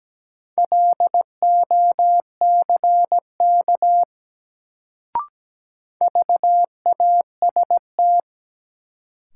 Morse code, heard as a keyed tone, about 17 wpm, 700 Hz. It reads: LOCK VAST
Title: Sentences from Top 1300 Words - New Word Drill - 17wpm